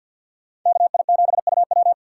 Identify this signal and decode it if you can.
Morse code, heard as a keyed tone, about 50 wpm, 700 Hz. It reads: KI7UW
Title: USA Call Signs - 5 Characters - 2x Character Spacing - 50wpm